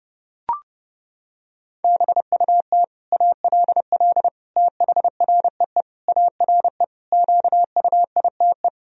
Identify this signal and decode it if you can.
Morse code, heard as a keyed tone, about 31 wpm, 700 Hz. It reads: BUT ALL THREE ARE QUITE
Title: Sentences from Top 1000 Words - Review Part 3 - 30wpm